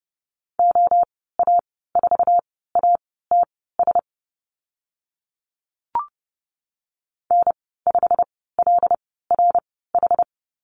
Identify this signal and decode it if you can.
Morse code, heard as a keyed tone, about 30 wpm, spaced as for 17 wpm, 700 Hz. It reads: OA4ATS N5LRH
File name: Lesson 15g - Number 5 - Call Signs 30wpm